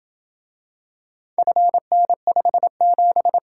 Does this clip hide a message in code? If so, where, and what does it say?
Yes: Morse code from 1.4 to 3.4 s, heard as a keyed tone, about 27 wpm, 700 Hz. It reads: FN57